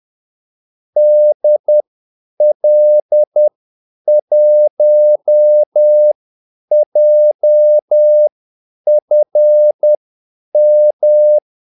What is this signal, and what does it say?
Morse code, heard as a keyed tone, about 10 wpm, spaced as for 8 wpm, 600 Hz. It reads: DL1JFM
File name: Callsigns-10-8wpm